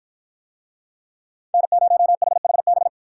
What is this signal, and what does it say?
Morse code, heard as a keyed tone, about 53 wpm, 700 Hz. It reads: N0HHB